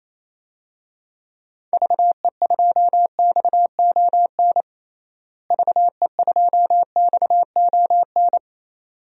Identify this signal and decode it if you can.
Morse code, heard as a keyed tone, about 27 wpm, 700 Hz. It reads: VE2XON VE2XON
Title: DXCC-100 Call Signs - 6 Characters - Repeated 2x - 28wpm